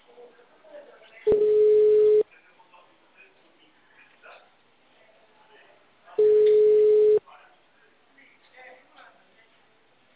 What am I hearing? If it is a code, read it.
Ringback tone (Europe, 425 Hz), with speech.